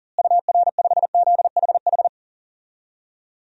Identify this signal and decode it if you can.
Morse code, heard as a keyed tone, about 40 wpm, 700 Hz. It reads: UR5ZHH
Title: DXCC-100 Call Signs - 6 Characters - Repeated 3x - 40wpm